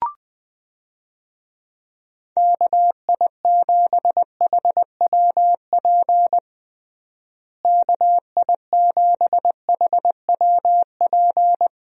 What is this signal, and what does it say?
Morse code, heard as a keyed tone, about 20 wpm, 700 Hz. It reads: KI7HWP KI7HWP
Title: USA Call Signs - 6 Characters - Repeated 3x - 20wpm